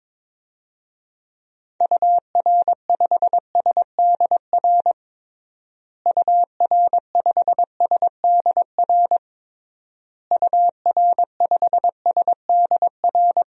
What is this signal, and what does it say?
Morse code, heard as a keyed tone, about 22 wpm, 700 Hz. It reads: UR5SDR UR5SDR UR5SDR
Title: DXCC-100 Call Signs - 6 Characters - Repeated 3x - 22wpm